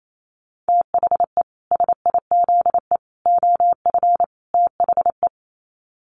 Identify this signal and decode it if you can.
Morse code, heard as a keyed tone, about 28 wpm, 700 Hz. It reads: THE SIZE OF THE
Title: Sets of 4 Words 28wpm